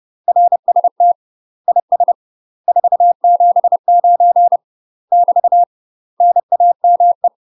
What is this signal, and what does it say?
Morse code, heard as a keyed tone, about 30 wpm, 700 Hz. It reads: RST IS 479 <BT> NAME
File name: UK MASTER QSOs TRAINING / 30WPM / UK MASTER QSOs 0028